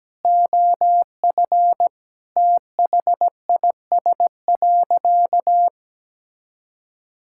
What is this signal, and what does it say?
Morse code, heard as a keyed tone, about 17 wpm, 700 Hz. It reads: OF THIS.
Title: Sentences from Top 1000 Words - Review Part 6 - 17wpm